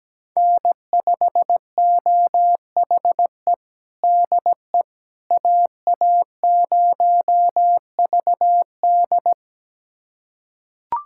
Morse code, heard as a keyed tone, about 17 wpm, 700 Hz. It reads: N5OHE DE AA0VD
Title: USA Callsign DE Callsign - 5 Characters - 17wpm